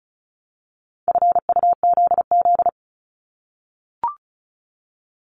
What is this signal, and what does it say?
Morse code, heard as a keyed tone, about 36 wpm, 700 Hz. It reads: FUZZ
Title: Multi-letter Phonemes 35wpm